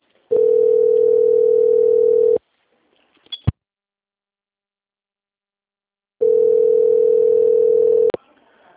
Ringback tone (North America), 440 + 480 Hz.